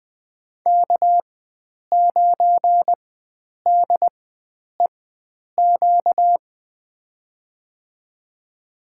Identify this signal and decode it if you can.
Morse code, heard as a keyed tone, about 20 wpm, spaced as for 9 wpm, 700 Hz. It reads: K9DEQ
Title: USA Call Signs - 5 Characters - 4x Character Spacing - 20wpm